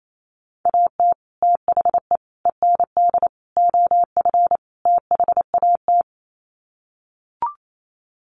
Morse code, heard as a keyed tone, about 28 wpm, 700 Hz. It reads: AT THE END OF THAT
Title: Sets of 5 Words Encore2 28wpm